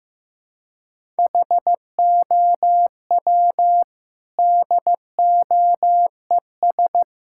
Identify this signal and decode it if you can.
Morse code, heard as a keyed tone, about 15 wpm, 700 Hz. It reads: HOW DOES
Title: Sentences from Top 1000 Words - Review Part 5 - 15wpm